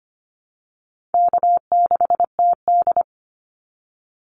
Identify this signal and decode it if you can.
Morse code, heard as a keyed tone, about 25 wpm, 700 Hz. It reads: K6TD
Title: Easy Call Signs - 25wpm